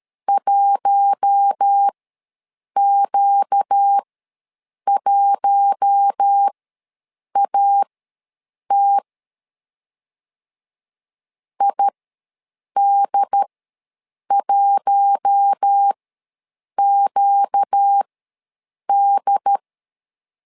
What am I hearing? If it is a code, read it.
Morse code, heard as a keyed tone, about 13 wpm, spaced as for 7 wpm, 800 Hz. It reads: 1Q1AT ID1QD
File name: LEZIONI / 4 Q1 ws 12wpm 2129